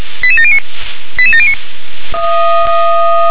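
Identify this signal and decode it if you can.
Five-tone selective call (ZVEI), heard as quick runs of five tones, tones of about 70 ms. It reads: 88709 88709
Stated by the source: ZVEI five-tone selective call received on SDR.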